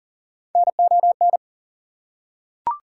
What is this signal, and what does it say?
Morse code, heard as a keyed tone, about 40 wpm, 700 Hz. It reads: NON
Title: Prefixes and Suffixes 40wpm